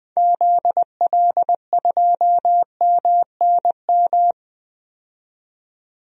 Morse code, heard as a keyed tone, about 20 wpm, 700 Hz. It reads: ZL2MNM
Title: DXCC-100 Call Signs - 6 Characters - 20wpm